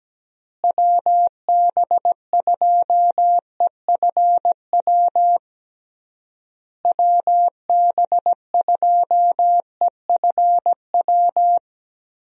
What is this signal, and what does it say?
Morse code, heard as a keyed tone, about 17 wpm, 700 Hz. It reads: WB2EFW WB2EFW
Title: USA Call Signs - 6 Characters - Repeated 3x - 17wpm